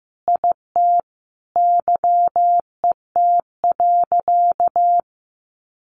Morse code, heard as a keyed tone, about 15 wpm, 700 Hz. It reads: IT YET.
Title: Sentences from Top 1100 Words 15wpm